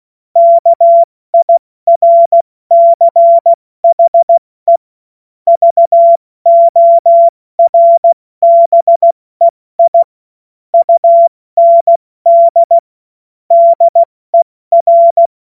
Morse code, heard as a keyed tone, about 16 wpm, 680 Hz. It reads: KIRCHE VORBEI UND DER
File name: Morsekurs / Witze / Witz 30. Dezember 2025 70 F80